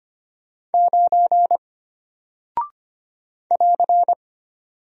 Morse code, heard as a keyed tone, about 25 wpm, 700 Hz. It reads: 9 <AR>
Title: Single Character (Letter:Number:Punctuation:Prosign) Alt 25wpm